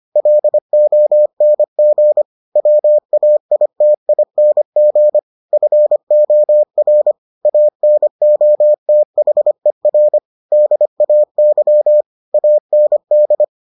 Morse code, heard as a keyed tone, about 25 wpm, 600 Hz. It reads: LONG WAITING FOR ANOTHER DAY AND